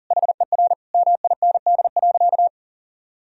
Morse code, heard as a keyed tone, about 40 wpm, 700 Hz. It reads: HER MIND.
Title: Sentences from Top 1000 Words - Review Part 4 - 40wpm